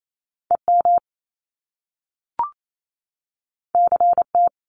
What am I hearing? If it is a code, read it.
Morse code, heard as a keyed tone, about 28 wpm, 700 Hz. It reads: EM CT
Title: Top Two Letter Combinations in Words 28wpm